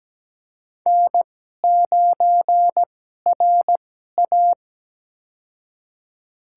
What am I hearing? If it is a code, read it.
Morse code, heard as a keyed tone, about 17 wpm, spaced as for 12 wpm, 700 Hz. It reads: N9RA
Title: USA Call Signs - 4 Characters - 2x Character Spacing - 17wpm